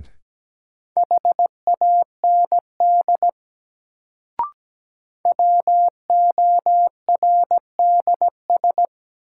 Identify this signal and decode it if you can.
Morse code, heard as a keyed tone, about 17 wpm, 700 Hz. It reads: HAND WORDS
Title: Sets of 3 Words - New Word Drill - 17wpm